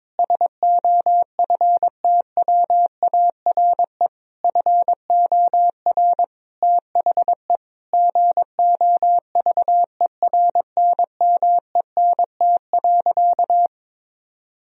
Morse code, heard as a keyed tone, about 22 wpm, 700 Hz. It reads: SOFTWARE FOR THE GOVERNMENT.
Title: Sentences from Top 1000 Words - Review Part 3 - 22wpm